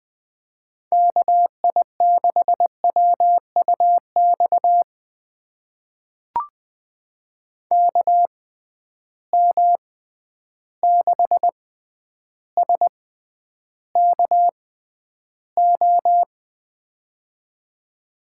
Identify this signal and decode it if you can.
Morse code, heard as a keyed tone, about 20 wpm, spaced as for 10 wpm, 700 Hz. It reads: KI6WUX KM6SKO
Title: USA Call Signs - 6x Character Spacing - 20wpm